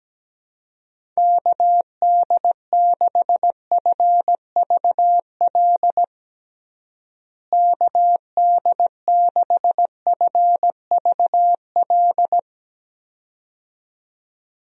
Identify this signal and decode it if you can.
Morse code, heard as a keyed tone, about 17 wpm, 700 Hz. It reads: KD6FVL KD6FVL
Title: USA Call Signs - 6 Characters - Repeated 2x - 17wpm